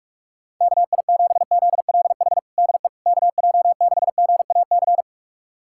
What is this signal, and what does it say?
Morse code, heard as a keyed tone, about 41 wpm, 690 Hz. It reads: KI7ZLH DE KJ6GAC
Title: USA Callsign DE Callsign - 6 Characters - 45wpm